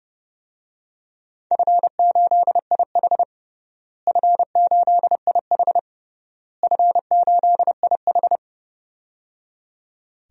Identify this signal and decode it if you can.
Morse code, heard as a keyed tone, about 30 wpm, 700 Hz. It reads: F8IH F8IH F8IH
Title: DXCC-100 Call Signs - 4 Characters - Repeated 3x - 30wpm